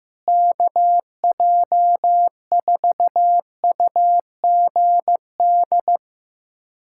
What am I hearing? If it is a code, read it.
Morse code, heard as a keyed tone, about 15 wpm, 700 Hz. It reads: KJ4UGD
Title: USA Callsign DE Callsign 15wpm